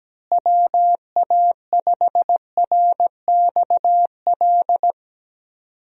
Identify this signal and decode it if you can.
Morse code, heard as a keyed tone, about 17 wpm, 700 Hz. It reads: WA5RXL
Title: USA Call Signs - 6 Characters - Repeated 3x - 17wpm